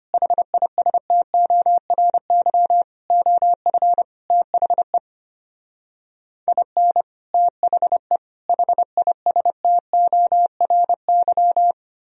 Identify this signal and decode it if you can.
Morse code, heard as a keyed tone, about 33 wpm, 700 Hz. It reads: HISTORY OF THE IN THE HISTORY